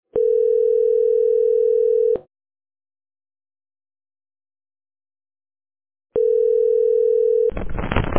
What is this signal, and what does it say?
Ringback tone (North America), 440 + 480 Hz.